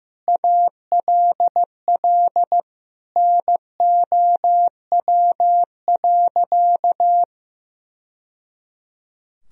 Morse code, heard as a keyed tone, about 15 wpm, 700 Hz. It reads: ALL NOW.
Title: Sentences from Top 1000 Words - Review Part 1 - 15wpm